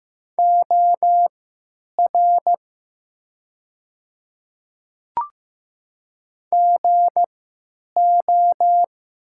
Morse code, heard as a keyed tone, about 15 wpm, spaced as for 9 wpm, 700 Hz. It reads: OR GO
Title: 3x Character Spacing - Top 100 Words - 15wpm